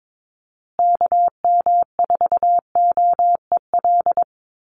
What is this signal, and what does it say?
Morse code, heard as a keyed tone, about 22 wpm, 700 Hz. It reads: KM4OEL
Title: USA Call Signs Repeated 2x 22wpm